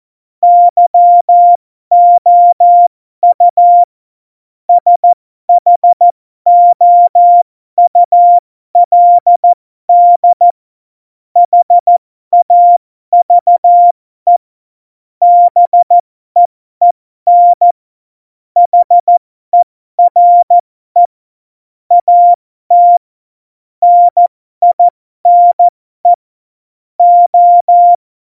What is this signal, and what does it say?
Morse code, heard as a keyed tone, about 14 wpm, spaced as for 12 wpm, 700 Hz. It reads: YOU SHOULD HAVE BEEN HERE AT NINE O
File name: Morsekurs / Witze